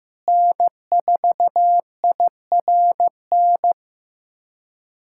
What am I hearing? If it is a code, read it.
Morse code, heard as a keyed tone, about 15 wpm, 700 Hz. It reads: N4IRN
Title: K1USN SST QSO Practice - 15wpm